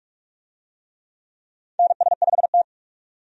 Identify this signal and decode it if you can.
Morse code, heard as a keyed tone, about 46 wpm, 700 Hz. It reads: NS5T
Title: USA Call Signs - 4 Characters - 45wpm